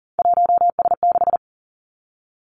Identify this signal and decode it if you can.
Morse code, heard as a keyed tone, about 38 wpm, 700 Hz. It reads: 1S6
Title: Three Challenging Letter-Number Combinations 40wpm